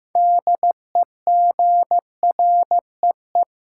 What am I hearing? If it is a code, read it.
Morse code, heard as a keyed tone, about 15 wpm, 700 Hz. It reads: DEGREE
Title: Sentences from Top 1500 Words - New Word Drill - 15wpm